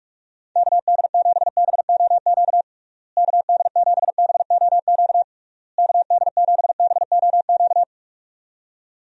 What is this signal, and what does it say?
Morse code, heard as a keyed tone, about 42 wpm, 700 Hz. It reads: KD7BOQ KD7BOQ KD7BOQ
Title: USA Call Signs - 6 Characters - Repeated 3x - 45wpm